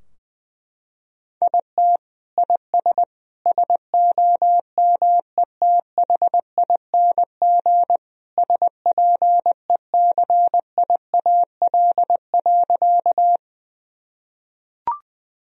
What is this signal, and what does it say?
Morse code, heard as a keyed tone, about 21 wpm, 700 Hz. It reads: IT IS SOMETHING SPECIAL.
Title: Sentences from Top 1000 Words - Review Part 3 - 20wpm